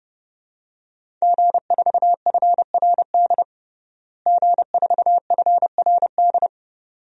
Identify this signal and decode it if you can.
Morse code, heard as a keyed tone, about 30 wpm, 700 Hz. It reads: G4FRD G4FRD
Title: DXCC-100 Call Signs - 5 Characters - Repeated 2x - 30wpm